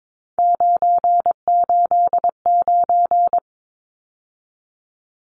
Morse code, heard as a keyed tone, about 22 wpm, 700 Hz. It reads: 989
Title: Numbers - 3-Digit 22wpm